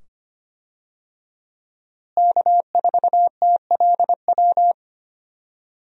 Morse code, heard as a keyed tone, about 25 wpm, 700 Hz. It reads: K4TLW
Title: USA Call Signs - 5 Characters - Repeated 2x - 25wpm